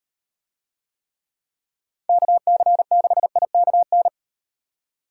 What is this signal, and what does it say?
Morse code, heard as a keyed tone, about 38 wpm, 700 Hz. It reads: KC6IKN